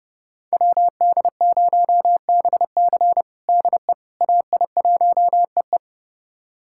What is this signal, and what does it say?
Morse code, heard as a keyed tone, about 30 wpm, 700 Hz. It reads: WD0BC DE AI1EE
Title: USA Callsign DE Callsign - 5 Characters - 30wpm